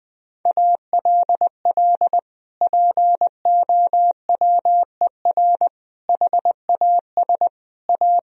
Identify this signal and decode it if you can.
Morse code, heard as a keyed tone, about 20 wpm, 700 Hz. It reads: ALL POWER HAS A
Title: Sentences from Top 1000 Words - Review Part 2 - 20wpm